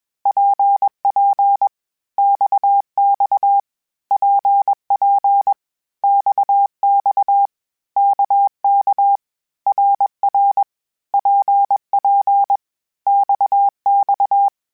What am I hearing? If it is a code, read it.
Morse code, heard as a keyed tone, about 21 wpm, 800 Hz. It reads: PP XX PP XX KK RR PP XX
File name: Groups@20 / Groups@20 mp3s / RKPX@20